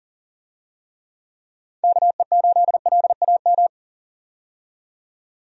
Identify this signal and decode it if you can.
Morse code, heard as a keyed tone, about 40 wpm, 700 Hz. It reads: KE8LAM
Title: USA Call Signs - 6 Characters - 40wpm